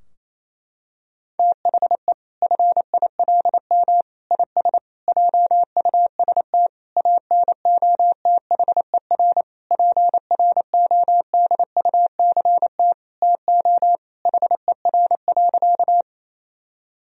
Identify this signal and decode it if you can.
Morse code, heard as a keyed tone, about 28 wpm, 700 Hz. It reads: THE FILM IS JUST ANOTHER PRODUCT TO HER.
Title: Sentences from Top 1000 Words - Review Part 3 - 28wpm